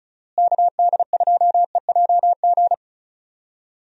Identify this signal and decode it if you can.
Morse code, heard as a keyed tone, about 34 wpm, 700 Hz. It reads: KD2EJG
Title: USA Call Signs - 6 Characters - Repeated 3x - 35wpm